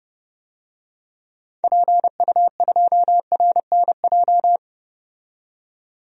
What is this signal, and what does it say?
Morse code, heard as a keyed tone, about 30 wpm, 700 Hz. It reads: PU2RNJ
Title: DXCC-100 Call Signs - 6 Characters - 30wpm